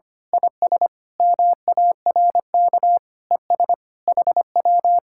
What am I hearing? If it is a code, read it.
Morse code, heard as a keyed tone, about 24 wpm, 700 Hz. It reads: IS MARK ES HW
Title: Speed-Racing Common QSO Phrases Encore - 25wpm